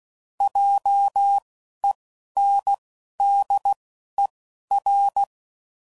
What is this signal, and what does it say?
Morse code, heard as a keyed tone, about 16 wpm, 800 Hz. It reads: J E N D E R